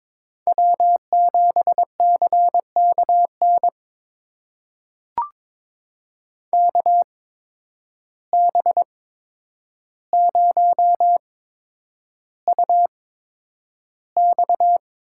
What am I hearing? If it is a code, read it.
Morse code, heard as a keyed tone, about 22 wpm, spaced as for 9 wpm, 700 Hz. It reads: W7CKN KB0UX
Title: USA Call Signs - 5 Characters - 8x Character Spacing - 22wpm